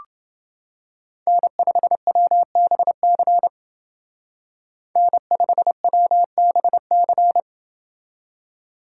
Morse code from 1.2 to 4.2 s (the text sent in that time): N5WBC